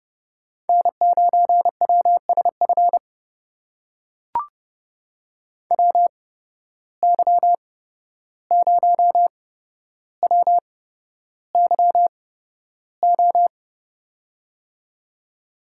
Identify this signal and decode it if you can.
Morse code, heard as a keyed tone, about 30 wpm, spaced as for 12 wpm, 700 Hz. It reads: N9WSF WY0WYO
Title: USA Call Signs - 8x Character Spacing - 30wpm